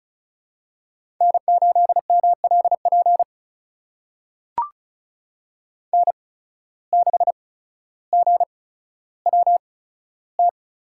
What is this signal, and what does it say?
Morse code, heard as a keyed tone, about 36 wpm, spaced as for 15 wpm, 700 Hz. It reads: N8MLP N6GWT
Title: USA Call Signs - 5 Characters - 8x Character Spacing - 35wpm